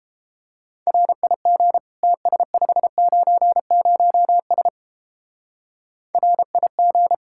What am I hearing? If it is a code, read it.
Morse code, heard as a keyed tone, about 31 wpm, 700 Hz. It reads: RIG TS590S RIG